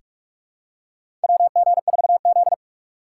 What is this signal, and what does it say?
Morse code, heard as a keyed tone, about 48 wpm, 700 Hz. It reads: WG4Z
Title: USA Call Signs - 3x Character Spacing - 45wpm